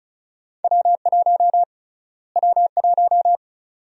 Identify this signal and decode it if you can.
Morse code, heard as a keyed tone, about 34 wpm, 700 Hz. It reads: W1 W1
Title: USA Call Sign Prefixes - Repeated 2x - 35wpm